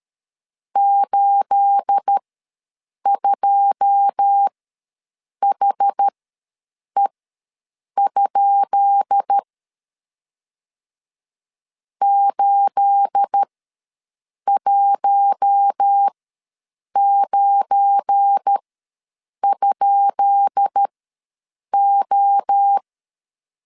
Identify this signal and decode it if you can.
Morse code, heard as a keyed tone, about 13 wpm, spaced as for 7 wpm, 800 Hz. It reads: 82HE? 819?O
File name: LEZIONI / 10 SIGNS / 10B INT ws 12wpm 3527